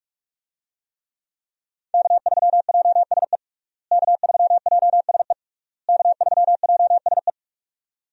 Morse code, heard as a keyed tone, about 42 wpm, 700 Hz. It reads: K3JSE K3JSE K3JSE
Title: USA Call Signs - 5 Characters - Repeated 3x - 45wpm